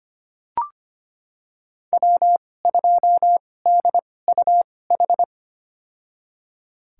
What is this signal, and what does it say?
Morse code, heard as a keyed tone, about 25 wpm, spaced as for 18 wpm, 700 Hz. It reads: W2DUH
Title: USA Call Signs - 5 Characters - 2x Character Spacing - 25wpm